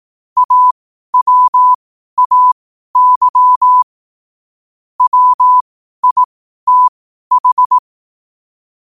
Morse code, heard as a keyed tone, about 18 wpm, spaced as for 12 wpm, 1000 Hz. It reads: AWAY WITH
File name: qotdfiles / 12wpm